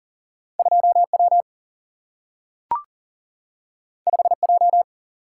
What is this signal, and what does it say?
Morse code, heard as a keyed tone, about 40 wpm, 700 Hz. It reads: 2 W 5 J
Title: Single-Letter Single-Number 40wpm